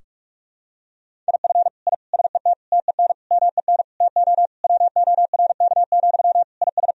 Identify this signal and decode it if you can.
Morse code, heard as a keyed tone, about 46 wpm, 700 Hz. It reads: IF I SET TEN MEN TO WORK, IS
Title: Sentences from Top 1000 Words - Review Part 2 - 45wpm